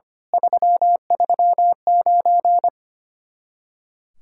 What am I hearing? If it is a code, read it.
Morse code, heard as a keyed tone, about 25 wpm, 700 Hz. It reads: 339
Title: Speed-Racing Common QSO Phrases - 25wpm